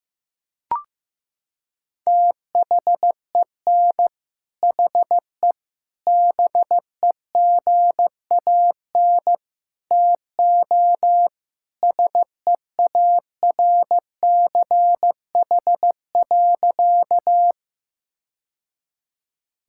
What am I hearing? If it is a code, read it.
Morse code, heard as a keyed tone, about 15 wpm, 700 Hz. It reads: THEN HE BEGAN TO SEARCH.
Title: Sentences from Top 1000 Words - Review Part 5 - 15wpm